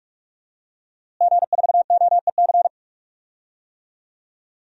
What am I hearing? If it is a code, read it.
Morse code, heard as a keyed tone, about 46 wpm, 700 Hz. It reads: G4OEC
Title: DXCC-100 Call Signs - 5 Characters - Repeated 2x - 45wpm